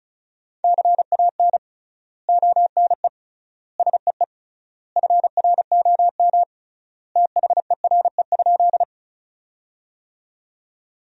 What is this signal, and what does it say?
Morse code, heard as a keyed tone, about 36 wpm, 700 Hz. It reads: CAN ONE SEE FROM THERE?
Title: 3x Word-Spacing - Sentences from Top 100 Words - 35wpm